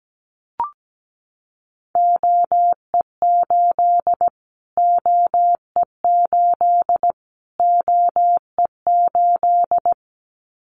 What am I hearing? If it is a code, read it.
Morse code, heard as a keyed tone, about 17 wpm, 700 Hz. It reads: OE8 OE8 OE8
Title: Callsign Prefixes for Top 25 DXCC - Repeated 3x - 17wpm